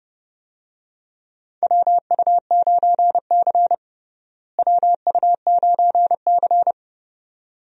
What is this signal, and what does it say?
Morse code, heard as a keyed tone, about 29 wpm, 700 Hz. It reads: WU9C WU9C
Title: USA Call Signs - 4 Characters - Repeated 2x - 30wpm